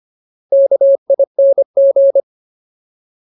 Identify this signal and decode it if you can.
Morse code, heard as a keyed tone, about 25 wpm, 550 Hz. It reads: KING
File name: Ultimate Word List / 25WPM Ultimate